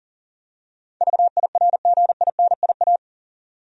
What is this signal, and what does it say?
Morse code, heard as a keyed tone, about 38 wpm, 700 Hz. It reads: VIRGINIA
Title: USA Common Names - 40wpm